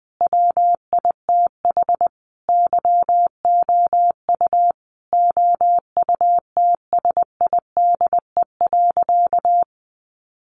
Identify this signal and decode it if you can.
Morse code, heard as a keyed tone, about 20 wpm, 700 Hz. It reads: WITH YOU OUTSIDE.
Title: Sentences from Top 900 Words 20wpm